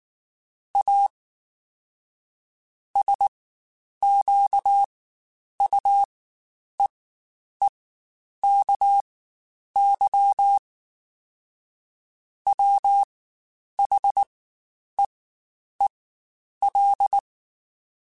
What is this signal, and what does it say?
Morse code, heard as a keyed tone, about 19 wpm, spaced as for 9 wpm, 800 Hz. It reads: A SQUEEKY WHEEL
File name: Spaced / Chuckles Spaced-mp3s / Squeaky sp @18